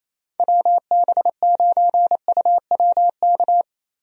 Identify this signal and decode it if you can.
Morse code, heard as a keyed tone, about 28 wpm, 700 Hz. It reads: WB9UWK
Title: USA Call Signs - 6 Characters - Repeated 2x - 28wpm